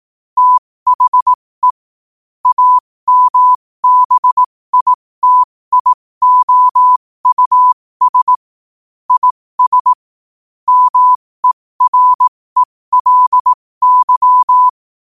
Morse code, heard as a keyed tone, about 18 wpm, spaced as for 15 wpm, 1000 Hz. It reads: THE AMBITIOUS IS MERELY
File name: qotdfiles / 15wpm